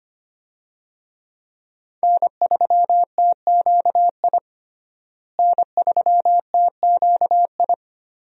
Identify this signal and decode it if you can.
Morse code, heard as a keyed tone, about 25 wpm, 690 Hz. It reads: N3TQI N3TQI
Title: USA Call Signs - 5 Characters - Repeated 2x - 25wpm